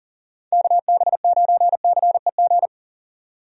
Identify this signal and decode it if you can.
Morse code, heard as a keyed tone, about 40 wpm, 700 Hz. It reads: KB9CEG
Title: USA Call Signs - 6 Characters - 40wpm